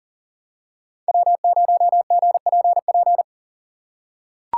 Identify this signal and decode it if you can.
Morse code, heard as a keyed tone, about 40 wpm, 700 Hz. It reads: W0GPP
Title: USA Call Signs - 5 Characters - 6x Character Spacing - 40wpm